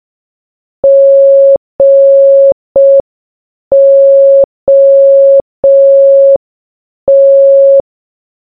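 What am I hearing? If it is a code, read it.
Morse code, heard as a keyed tone, about 5 wpm, 550 Hz. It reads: GOT
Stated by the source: Morse sound from the game Baba Is You.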